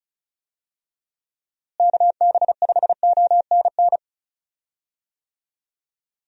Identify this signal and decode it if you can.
Morse code, heard as a keyed tone, about 36 wpm, 700 Hz. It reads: KB5ONN
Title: USA Call Signs - 6 Characters - Repeated 3x - 35wpm